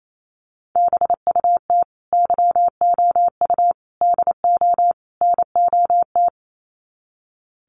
Morse code, heard as a keyed tone, about 28 wpm, 700 Hz. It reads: BUT YOU DO NOT
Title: Sets of 4 Words 28wpm